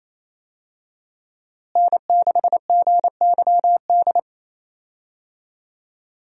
Morse code, heard as a keyed tone, about 28 wpm, 700 Hz. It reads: N6GYD
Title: USA Call Signs - 5 Characters - Repeated 2x - 28wpm